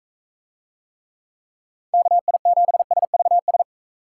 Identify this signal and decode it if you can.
Morse code, heard as a keyed tone, about 41 wpm, 700 Hz. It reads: KI7SVS